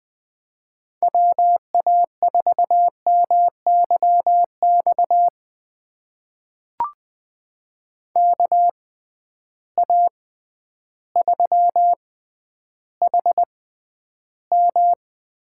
Morse code, heard as a keyed tone, about 20 wpm, spaced as for 11 wpm, 700 Hz. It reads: WA4MYX KA3HM
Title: USA Call Signs - 6 Characters - 6x Character Spacing - 20wpm